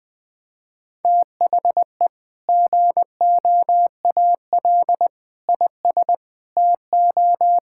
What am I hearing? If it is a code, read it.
Morse code, heard as a keyed tone, about 21 wpm, 700 Hz. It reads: THE GOAL IS TO
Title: Sentences with 2-Word Phrases - 20wpm